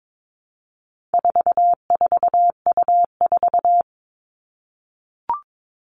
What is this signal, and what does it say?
Morse code, heard as a keyed tone, about 22 wpm, 700 Hz. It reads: 44U4
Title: All Cut Numbers - 4-Digit 22wpm